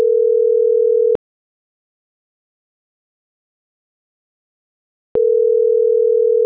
Ringback tone (North America), 440 + 480 Hz.